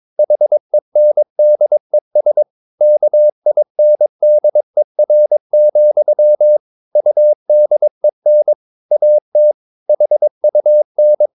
Morse code, heard as a keyed tone, about 22 wpm, 600 Hz. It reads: HENDES KINDER, UDEN AT HUN
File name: Ferie QRS 22wpm